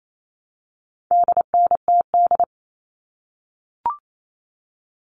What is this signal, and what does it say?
Morse code, heard as a keyed tone, about 28 wpm, 700 Hz. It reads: DNTD
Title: All Cut Numbers - 4-Digit 28wpm